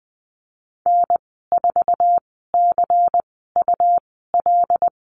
Morse code, heard as a keyed tone, about 20 wpm, spaced as for 14 wpm, 700 Hz. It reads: N4CUL
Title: 161 Lesson 26g - Letter g - Call Signs 20wpm